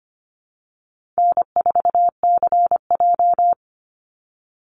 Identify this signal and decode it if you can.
Morse code, heard as a keyed tone, about 25 wpm, 700 Hz. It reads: N4CJ
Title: Call Signs - 25wpm